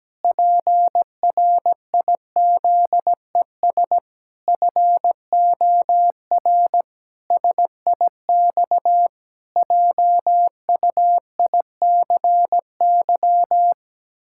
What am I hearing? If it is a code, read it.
Morse code, heard as a keyed tone, about 17 wpm, 700 Hz. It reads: PRIZES FOR SIX JUICY
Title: Pangram Challenge v2 - 17wpm